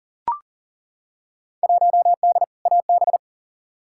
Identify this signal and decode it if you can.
Morse code, heard as a keyed tone, about 37 wpm, 700 Hz. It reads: 1D AB
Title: Field Day Exchanges Encore - 40wpm